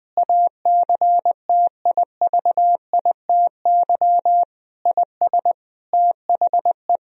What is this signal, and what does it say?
Morse code, heard as a keyed tone, about 21 wpm, 700 Hz. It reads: ACTIVITY IS THE
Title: Sentences from Top 1000 Words - Review Part 4 - 20wpm